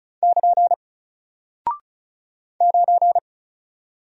Morse code, heard as a keyed tone, about 36 wpm, 700 Hz. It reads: <KN> 9
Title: Single Character (Letter:Number:Punctuation:Prosign) Alt 35wpm